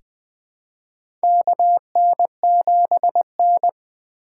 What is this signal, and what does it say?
Morse code, heard as a keyed tone, about 20 wpm, 700 Hz. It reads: KN7N
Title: USA Call Signs - 4x Character Spacing - 20wpm